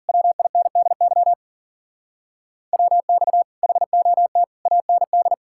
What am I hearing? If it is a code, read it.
Morse code, heard as a keyed tone, about 53 wpm, 700 Hz. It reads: WINDY WX HOT AND